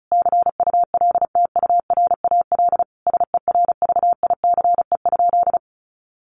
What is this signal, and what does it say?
Morse code, heard as a keyed tone, about 34 wpm, 700 Hz. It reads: CULTURAL SERVICE?
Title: Sentences from Top 1500 Words 35wpm